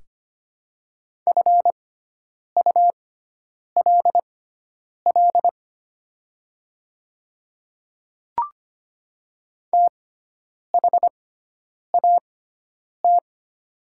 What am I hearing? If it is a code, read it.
Morse code, heard as a keyed tone, about 25 wpm, spaced as for 9 wpm, 700 Hz. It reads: FULL THAT
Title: 6x Character Spacing - Top 500 Words - 25wpm